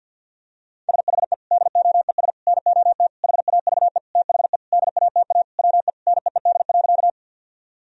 Morse code, heard as a keyed tone, about 50 wpm, 700 Hz. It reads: SHE DOES NOT HAVE THE DATA WE NEED.
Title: Sentences from Top 1000 Words - Review Part 3 - 50wpm